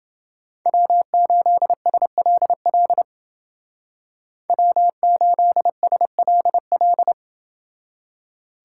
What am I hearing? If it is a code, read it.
Morse code, heard as a keyed tone, about 29 wpm, 700 Hz. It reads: W8SLL W8SLL